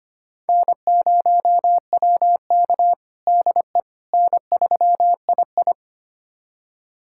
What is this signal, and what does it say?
Morse code, heard as a keyed tone, about 25 wpm, 700 Hz. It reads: N0WK DE N3II
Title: USA Callsign DE Callsign - 4 Characters - 25wpm